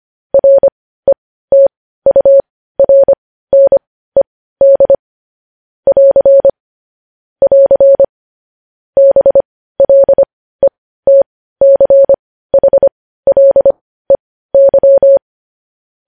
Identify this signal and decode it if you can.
Morse code, heard as a keyed tone, about 25 wpm, spaced as for 15 wpm, 550 Hz. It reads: RETURNED <AR> <AR> BLETCHLEY
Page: news headlines in morse code at 15 wpm